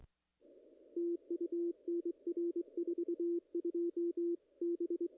Morse code, heard as a keyed tone, about 22 wpm, 340 Hz. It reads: TU NR 42 B
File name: contests / ss-cw-k5tr-op-n5rz -0000